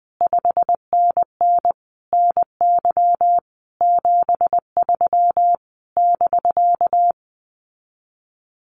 Morse code, heard as a keyed tone, about 20 wpm, 700 Hz. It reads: 5NN NY 73 <BK>
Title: Common QSO Phrases Encore 20wpm